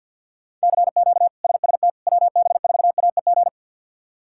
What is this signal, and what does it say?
Morse code, heard as a keyed tone, about 47 wpm, 700 Hz. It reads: CQ SST WB4AEG
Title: K1USN SST QSO Practice - 50wpm